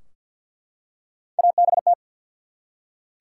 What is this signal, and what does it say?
Morse code, heard as a keyed tone, about 48 wpm, 700 Hz. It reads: ABT
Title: Top 180 QSO Elements 50wpm